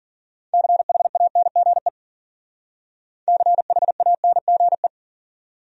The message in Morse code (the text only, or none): CHANGE CHANGE